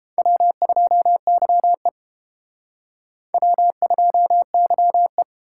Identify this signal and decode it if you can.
Morse code, heard as a keyed tone, about 29 wpm, 700 Hz. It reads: W2YE W2YE